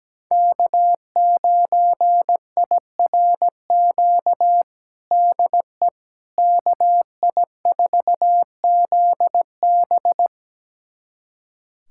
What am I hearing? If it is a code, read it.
Morse code, heard as a keyed tone, about 17 wpm, 700 Hz. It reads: K9IRQ DE KI4ZB
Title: USA Callsign DE Callsign - 5 Characters - 17wpm